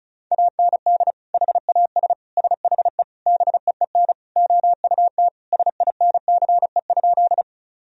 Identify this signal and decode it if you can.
Morse code, heard as a keyed tone, about 34 wpm, 700 Hz. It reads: AND HAS SHE BEEN OUT SINCE?
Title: Sentences from Top 1000 Words - Review Part 1 - 35wpm